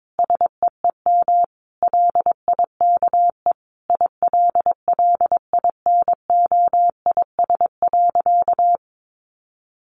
Morse code, heard as a keyed tone, about 22 wpm, 700 Hz. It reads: SEEM LIKE ILLINOIS.
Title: A Sound Of Thunder 22wpm